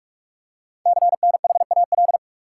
Morse code, heard as a keyed tone, about 44 wpm, 700 Hz. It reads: CNHAL
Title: Code Groups - 5 Letters - 45wpm